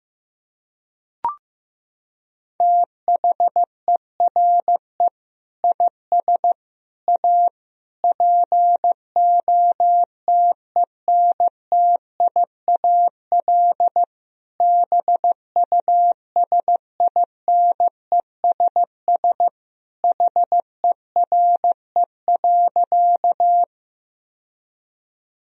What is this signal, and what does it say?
Morse code, heard as a keyed tone, about 16 wpm, 700 Hz. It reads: THERE IS A POTENTIAL BUSINESS HERE.
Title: Sentences from Top 1000 Words - Review Part 4 - 15wpm